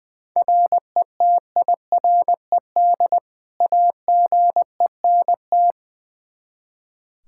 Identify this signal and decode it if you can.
Morse code, heard as a keyed tone, about 20 wpm, 700 Hz. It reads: RETIRED AGENT
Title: K1USN SST QSO Practice 20wpm